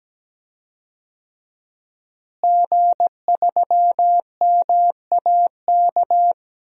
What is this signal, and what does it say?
Morse code, heard as a keyed tone, about 17 wpm, 700 Hz. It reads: G3MAK
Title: DXCC-100 Call Signs - 5 Characters - Repeated 2x - 17wpm